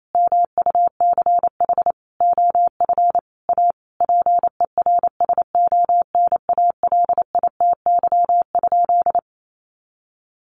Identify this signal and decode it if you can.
Morse code, heard as a keyed tone, about 28 wpm, 700 Hz. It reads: MUCH OF A PERSONALITY?
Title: Sentences from Top 1400 Words 28wpm